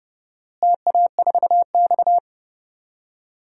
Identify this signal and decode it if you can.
Morse code, heard as a keyed tone, about 32 wpm, 700 Hz. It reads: TA4X
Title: DXCC-100 Call Signs - 4 Characters - Repeated 2x - 30wpm